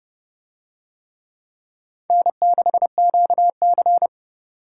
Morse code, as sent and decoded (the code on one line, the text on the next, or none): -. -.... --.- -.-.
N6QC